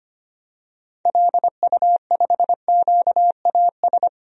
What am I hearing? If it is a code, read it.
Morse code, heard as a keyed tone, about 25 wpm, 700 Hz. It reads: LU5QAS
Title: DXCC-100 Call Signs - 6 Characters - 25wpm